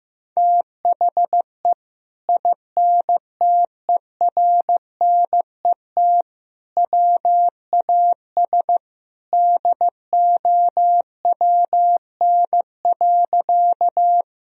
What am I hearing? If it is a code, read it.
Morse code, heard as a keyed tone, about 15 wpm, 700 Hz. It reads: THE INTERNET WAS DOWN.
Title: Sentences from Top 1000 Words - Review Part 4 - 15wpm